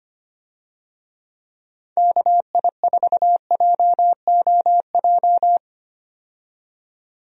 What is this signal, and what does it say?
Morse code, heard as a keyed tone, about 25 wpm, 700 Hz. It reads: KI4JOJ
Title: USA Call Signs - 6 Characters - 25wpm